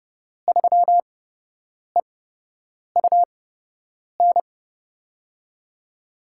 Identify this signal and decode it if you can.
Morse code, heard as a keyed tone, about 31 wpm, spaced as for 8 wpm, 700 Hz. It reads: 3EUN
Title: USA Call Signs - 5 Characters - 8x Character Spacing - 30wpm